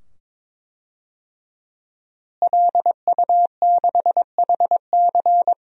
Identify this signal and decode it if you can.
Morse code, heard as a keyed tone, about 21 wpm, 700 Hz. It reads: LU6HC
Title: DXCC-100 Call Signs - 5 Characters - Repeated 2x - 22wpm